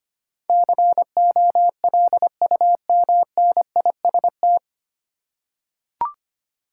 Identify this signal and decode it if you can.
Morse code, heard as a keyed tone, about 25 wpm, 700 Hz. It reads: COLUMNIST
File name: Nine Letter Words - 25wpm